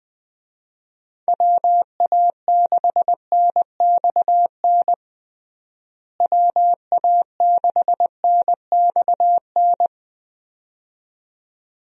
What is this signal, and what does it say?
Morse code, heard as a keyed tone, about 21 wpm, 700 Hz. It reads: WA6NXN WA6NXN
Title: USA Call Signs - 6 Characters - Repeated 2x - 20wpm